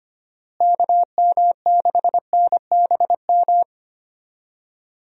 Morse code, heard as a keyed tone, about 25 wpm, 700 Hz. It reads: KM6NBM
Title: USA Call Signs - 6 Characters - Repeated 2x - 25wpm